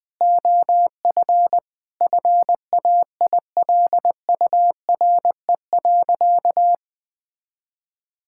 Morse code, heard as a keyed tone, about 21 wpm, 700 Hz. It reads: OF FAILURE.
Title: Sentences from Top 1000 Words - Review Part 6 - 20wpm